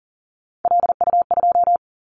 Morse code, heard as a keyed tone, about 37 wpm, 700 Hz. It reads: LU2
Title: Callsign Prefixes for Top 100 DXCC - 40wpm